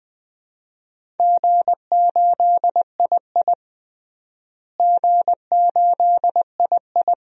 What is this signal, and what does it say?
Morse code, heard as a keyed tone, about 21 wpm, 700 Hz. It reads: G8II G8II
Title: DXCC-100 Call Signs - 4 Characters - Repeated 2x - 20wpm